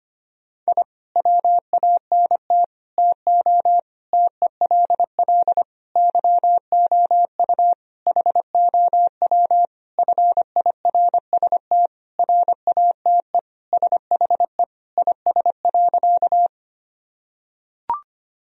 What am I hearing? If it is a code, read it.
Morse code, heard as a keyed tone, about 25 wpm, 700 Hz. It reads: I WANT TO TELL YOU HOW FIRST RATE SHE IS.
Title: Sentences from Top 1000 Words - Review Part 4 - 25wpm